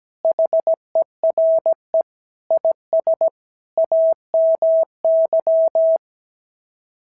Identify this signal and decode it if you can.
Morse code, heard as a keyed tone, about 16 wpm, 650 Hz. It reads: HERE IS AMY
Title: SOTA QSO Practice 17wpm